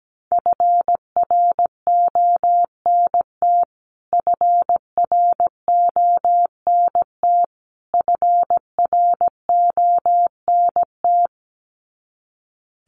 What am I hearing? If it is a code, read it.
Morse code, heard as a keyed tone, about 17 wpm, 700 Hz. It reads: FRONT FRONT FRONT
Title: Top 500 Words Repeated 17wpm